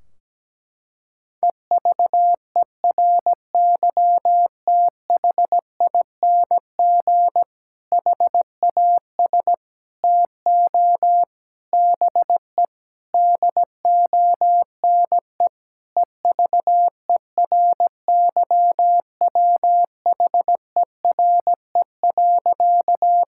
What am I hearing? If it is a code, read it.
Morse code, heard as a keyed tone, about 17 wpm, 700 Hz. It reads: EVERYTHING HAS TO BE DONE EVERYWHERE.
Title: Sentences from Top 1600 Words 17wpm